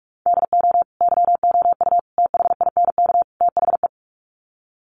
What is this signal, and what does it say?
Morse code, heard as a keyed tone, about 44 wpm, 700 Hz. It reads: DO YOU THINK THE
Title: Sets of 4 Words 45wpm